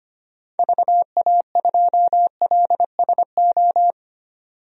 Morse code, heard as a keyed tone, about 25 wpm, 700 Hz. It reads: VA2LSO
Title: DXCC-100 Call Signs - 6 Characters - 25wpm